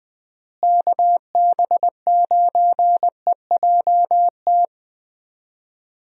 Morse code, heard as a keyed tone, about 20 wpm, 700 Hz. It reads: KB9EJT
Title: USA Call Signs - 6 Characters - Repeated 3x - 20wpm